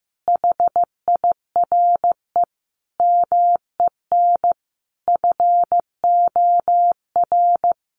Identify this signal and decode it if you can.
Morse code, heard as a keyed tone, about 15 wpm, 700 Hz. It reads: HIRE MEN FOR
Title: Sentences from Top 1400 Words 15wpm